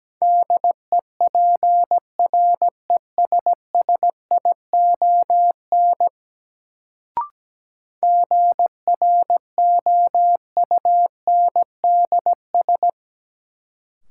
Morse code, heard as a keyed tone, about 17 wpm, 700 Hz. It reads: DEPRESSION GROUNDS
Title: Sentences from Top 1300 Words - New Word Drill - 17wpm